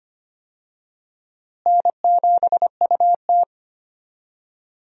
Morse code, heard as a keyed tone, about 25 wpm, 700 Hz. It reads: N7UT
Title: USA Call Signs - 4 Characters - Repeated 2x - 25wpm